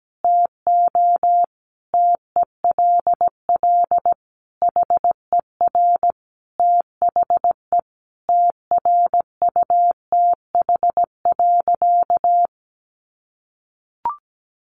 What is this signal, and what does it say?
Morse code, heard as a keyed tone, about 17 wpm, 700 Hz. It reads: TO TELL HER THE TRUTH.
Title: Sentences from Top 800 Words 17wpm